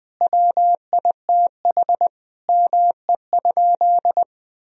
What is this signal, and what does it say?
Morse code, heard as a keyed tone, about 20 wpm, 700 Hz. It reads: WITH ME?
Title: Sentences from Top 1000 Words - Review Part 1 - 20wpm